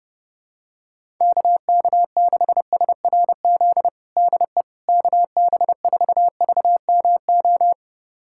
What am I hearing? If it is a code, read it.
Morse code, heard as a keyed tone, about 30 wpm, 700 Hz. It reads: KK6SRZ DE KB4VMO
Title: USA Callsign DE Callsign - 6 Characters - 30wpm